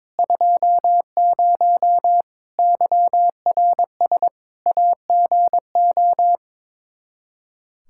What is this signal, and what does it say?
Morse code, heard as a keyed tone, about 22 wpm, 700 Hz. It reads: 20 YRS AGO
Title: Practice QSO - Exchange 3 - More Info - 22wpm